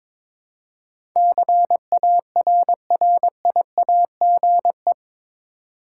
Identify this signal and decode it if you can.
Morse code, heard as a keyed tone, about 21 wpm, 700 Hz. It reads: CARRIAGE
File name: practice / 22wpm / Eight Letter Words - 22wpm